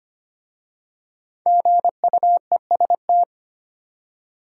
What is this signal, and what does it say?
Morse code, heard as a keyed tone, about 25 wpm, 700 Hz. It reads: GUEST
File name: 5 Letter Words - First 3 Letters Given - 25wpm